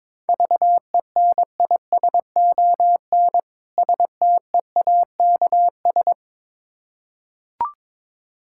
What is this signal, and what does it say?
Morse code, heard as a keyed tone, about 22 wpm, 690 Hz. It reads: VENISON STEAKS
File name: Difficult Words with Hints - 22wpm